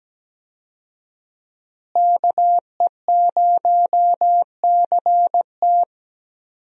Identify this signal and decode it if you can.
Morse code, heard as a keyed tone, about 17 wpm, 700 Hz. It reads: KE0CT
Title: USA Call Signs - 5 Characters - Repeated 3x - 17wpm